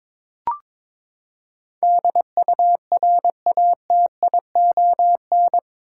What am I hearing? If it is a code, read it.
Morse code, heard as a keyed tone, about 22 wpm, 700 Hz. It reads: DURATION
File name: practice / 22wpm / Eight Letter Words - 22wpm